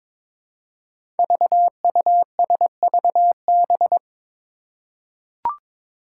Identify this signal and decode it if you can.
Morse code, heard as a keyed tone, about 22 wpm, 700 Hz. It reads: VUSVB
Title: Five Challenging Letter-Number Combinations 22wpm